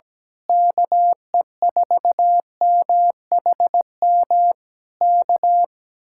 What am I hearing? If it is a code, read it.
Morse code, heard as a keyed tone, about 17 wpm, 700 Hz. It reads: KE4MHM K
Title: Speed-Racing Common QSO Phrases - 17wpm